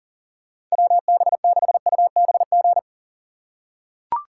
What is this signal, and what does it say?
Morse code, heard as a keyed tone, about 40 wpm, 700 Hz. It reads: WB6UBG
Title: USA Call Signs - 6 Characters - 2x Character Spacing - 40wpm